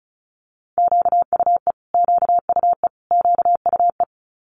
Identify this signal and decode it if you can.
Morse code, heard as a keyed tone, about 36 wpm, 700 Hz. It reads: QUE QUE QUE
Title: Multi-letter Phonemes Repeated 3x 35wpm